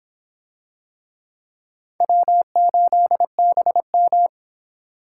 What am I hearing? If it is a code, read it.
Morse code, heard as a keyed tone, about 26 wpm, 700 Hz. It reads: W8BM